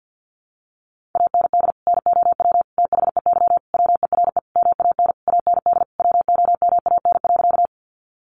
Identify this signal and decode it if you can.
Morse code, heard as a keyed tone, about 50 wpm, 700 Hz. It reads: AND NOW THEY WERE MAN AND WOMAN.
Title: Sentences from Top 1100 Words 50wpm